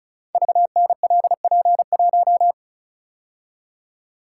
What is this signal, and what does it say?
Morse code, heard as a keyed tone, about 36 wpm, 700 Hz. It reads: VNLP1
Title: Code Groups - 5 Letter:Numbers - 35wpm